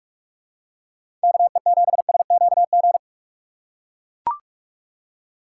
Morse code, heard as a keyed tone, about 43 wpm, 700 Hz. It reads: KE7SQG
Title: USA Call Signs - 6 Characters - 3x Character Spacing - 45wpm